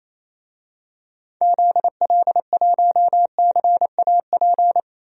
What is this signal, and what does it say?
Morse code, heard as a keyed tone, about 28 wpm, 700 Hz. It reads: ZL1CAP
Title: DXCC-100 Call Signs - 6 Characters - Repeated 2x - 28wpm